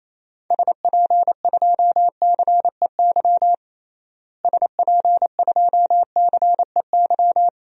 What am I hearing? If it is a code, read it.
Morse code, heard as a keyed tone, about 28 wpm, 700 Hz. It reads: SP2CEY SP2CEY
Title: DXCC-100 Call Signs - 6 Characters - Repeated 3x - 28wpm